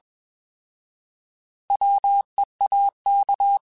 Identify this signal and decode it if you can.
Morse code, heard as a keyed tone, about 21 wpm, 800 Hz. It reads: WEAK
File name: Alphabetical / Alphabetical mp3s / Ws @20